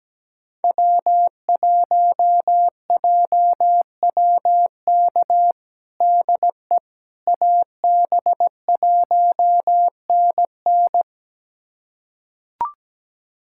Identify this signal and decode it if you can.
Morse code, heard as a keyed tone, about 17 wpm, 700 Hz. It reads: W1JWK DE AB1NN
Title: USA Callsign DE Callsign - 5 Characters - 17wpm